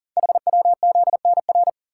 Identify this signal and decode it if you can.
Morse code, heard as a keyed tone, about 42 wpm, 700 Hz. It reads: HWZNR
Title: Code Groups - 5 Letter:Numbers - 40wpm